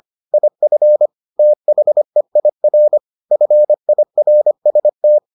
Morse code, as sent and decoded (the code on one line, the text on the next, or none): .. ..-. / - .... .. .-. / ..-. .. .-. ... -
IF THEIR FIRST